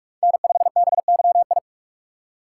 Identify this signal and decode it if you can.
Morse code, heard as a keyed tone, about 46 wpm, 700 Hz. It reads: N5BYI
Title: USA Call Signs - 3x Character Spacing - 45wpm